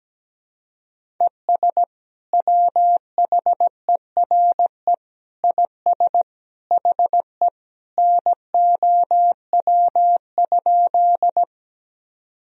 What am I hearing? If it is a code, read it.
Morse code, heard as a keyed tone, about 17 wpm, 700 Hz. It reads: ES WHERE IS HE NOW?
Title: Sentences from Top 200 Words and Abbreviations 17wpm